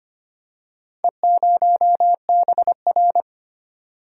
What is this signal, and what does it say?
Morse code, heard as a keyed tone, about 25 wpm, 700 Hz. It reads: E0BR
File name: practice / 25wpm / Four Letter-Number Combinations - 25wpm